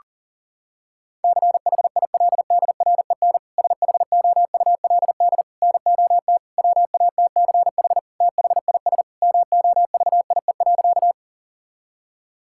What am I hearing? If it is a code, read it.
Morse code, heard as a keyed tone, about 40 wpm, 700 Hz. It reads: CHILDREN SHOULD NOT WATCH THIS MOVIE.
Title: Sentences from Top 1000 Words - Review Part 5 - 40wpm